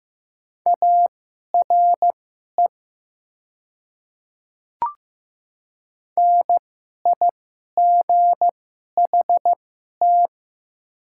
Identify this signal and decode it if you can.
Morse code, heard as a keyed tone, about 15 wpm, spaced as for 11 wpm, 700 Hz. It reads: ARE NIGHT
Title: 2x Character Spacing - Top 500 Words - 15wpm